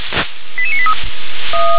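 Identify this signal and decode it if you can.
Five-tone selective call (ZVEI), heard as quick runs of five tones, tones of about 70 ms. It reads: 88983
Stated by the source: Five-tone ZVEI selective call received by SDR.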